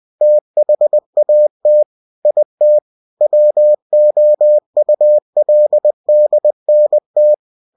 Morse code, heard as a keyed tone, about 20 wpm, 600 Hz. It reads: THAT IT WOULDNT
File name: cw-books / winnie-the-pooh / winnie-20wpm-page-0071